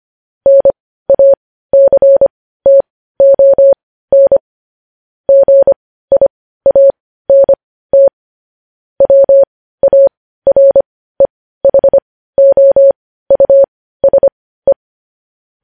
Morse code, heard as a keyed tone, about 25 wpm, spaced as for 15 wpm, 550 Hz. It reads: NACTON GIANT WAREHOUSE
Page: news-headlines-in-morse-code-at-15-wpm